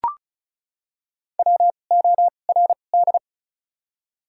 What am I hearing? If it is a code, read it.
Morse code, heard as a keyed tone, about 34 wpm, spaced as for 25 wpm, 700 Hz. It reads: WORD
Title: 2x Character Spacing - Top 200 Words - 35wpm